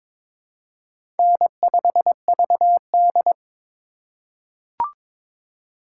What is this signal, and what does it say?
Morse code, heard as a keyed tone, about 22 wpm, 700 Hz. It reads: N5VD
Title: USA Call Signs - 4 Characters - 2x Character Spacing - 22wpm